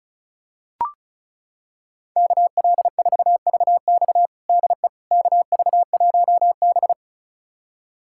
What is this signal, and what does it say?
Morse code, heard as a keyed tone, about 34 wpm, 700 Hz. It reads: KL4VX DE KV1B
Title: USA Callsign DE Callsign 35wpm